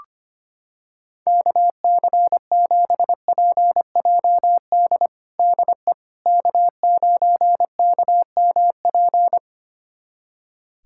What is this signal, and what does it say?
Morse code, heard as a keyed tone, about 25 wpm, 700 Hz. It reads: KC7PJD DE K9KMP
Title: USA Callsign DE Callsign 25wpm